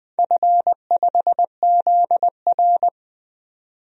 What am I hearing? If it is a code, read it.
Morse code, heard as a keyed tone, about 20 wpm, 700 Hz. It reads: F5ZR
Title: DXCC-100 Call Signs - 4 Characters - 20wpm